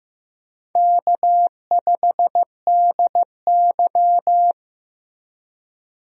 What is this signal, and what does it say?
Morse code, heard as a keyed tone, about 15 wpm, 700 Hz. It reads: K5DY
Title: USA Call Signs - 4 Characters - Repeated 3x - 15wpm